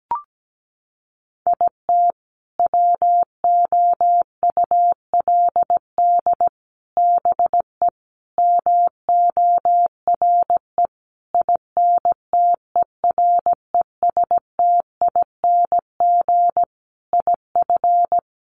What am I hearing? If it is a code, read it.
Morse code, heard as a keyed tone, about 17 wpm, 700 Hz. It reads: IT WOULD BE MORE INTERESTING IF
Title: Sentences from Top 1000 Words 17wpm